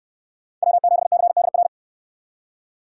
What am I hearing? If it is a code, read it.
Morse code, heard as a keyed tone, about 64 wpm, 700 Hz. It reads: W7CDR